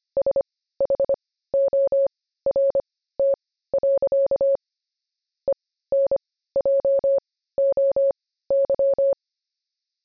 Morse code, heard as a keyed tone, about 25 wpm, spaced as for 15 wpm, 550 Hz. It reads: SHORT. ENJOY
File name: jb recaptured / morse